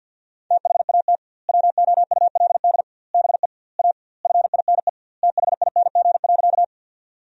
Morse code, heard as a keyed tone, about 50 wpm, 700 Hz. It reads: THAT WOULD BE A FINE THING.
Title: 2x Word-Spacing - Sentences from Top 500 Words - 50wpm